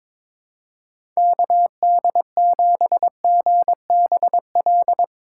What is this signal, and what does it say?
Morse code, heard as a keyed tone, about 22 wpm, 700 Hz. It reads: KD7GBL
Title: USA Call Signs - 6 Characters - 8x Character Spacing - 22wpm